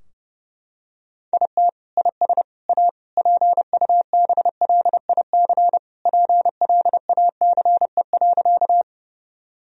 Morse code, heard as a keyed tone, about 30 wpm, 700 Hz. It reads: IT IS A PUBLIC PLACE.
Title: Sentences from Top 1000 Words - Review Part 3 - 30wpm